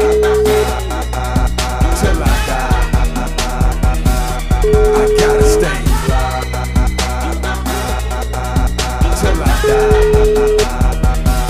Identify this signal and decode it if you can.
Ringback tone (Europe, 425 Hz), with noise.